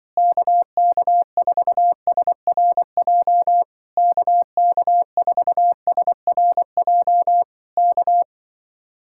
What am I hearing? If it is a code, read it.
Morse code, heard as a keyed tone, about 24 wpm, 700 Hz. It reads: KK4SRJ KK4SRJ K
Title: Speed-Racing Common QSO Phrases Encore - 20wpm